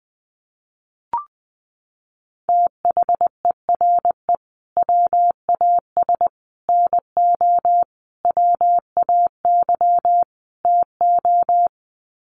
Morse code, heard as a keyed tone, about 20 wpm, 700 Hz. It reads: THERE WAS NO WAY TO
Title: Sets of 5 Words Encore 20wpm